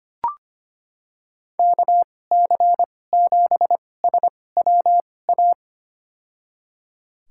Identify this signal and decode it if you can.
Morse code, heard as a keyed tone, about 25 wpm, spaced as for 18 wpm, 700 Hz. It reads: KC7SWA
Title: USA Call Signs - 6 Characters - 2x Character Spacing - 25wpm